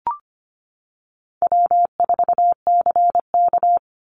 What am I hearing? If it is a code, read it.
Morse code, heard as a keyed tone, about 25 wpm, 700 Hz. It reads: W4CK
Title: Call Signs - 25wpm